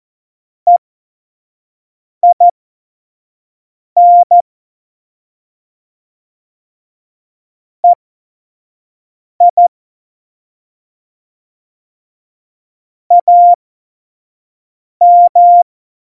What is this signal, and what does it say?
Morse code, heard as a keyed tone, about 14 wpm, spaced as for 5 wpm, 700 Hz. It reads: EIN EI AM
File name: Morsekurs / Witze / Witz 21 April 2020, 25 F70